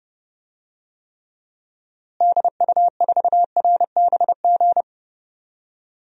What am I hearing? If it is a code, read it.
Morse code, heard as a keyed tone, about 32 wpm, 700 Hz. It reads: DU4RBG
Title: DXCC-100 Call Signs - 6 Characters - Repeated 3x - 30wpm